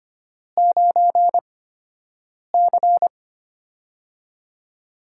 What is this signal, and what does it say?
Morse code, heard as a keyed tone, about 25 wpm, spaced as for 7 wpm, 700 Hz. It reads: 9C
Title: USA Call Signs - 4 Characters - 8x Character Spacing - 25wpm